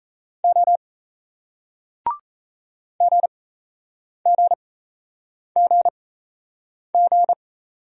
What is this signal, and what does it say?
Morse code, heard as a keyed tone, about 38 wpm, spaced as for 8 wpm, 700 Hz. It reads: O GGGG